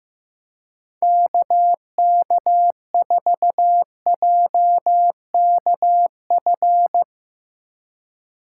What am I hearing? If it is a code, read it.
Morse code, heard as a keyed tone, about 15 wpm, 700 Hz. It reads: KK4JKF
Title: USA Call Signs - 6 Characters - 4x Character Spacing - 15wpm